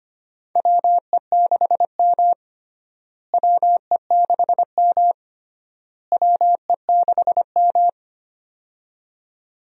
Morse code, heard as a keyed tone, about 25 wpm, 700 Hz. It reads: WE6M WE6M WE6M
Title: USA Call Signs - 4 Characters - Repeated 3x - 25wpm